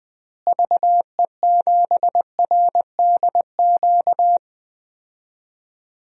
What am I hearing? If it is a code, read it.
Morse code, heard as a keyed tone, about 20 wpm, 700 Hz. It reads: VE7RDQ
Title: DXCC-100 Call Signs - 6 Characters - Repeated 3x - 20wpm